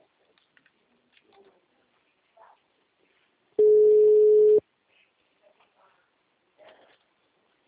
Ringback tone (Europe, 425 Hz).